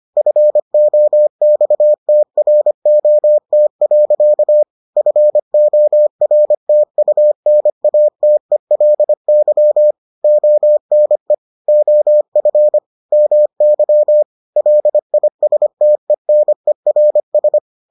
Morse code, heard as a keyed tone, about 25 wpm, 600 Hz. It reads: FOXTROT. FORTUNATELY ONE OF MY LISTENERS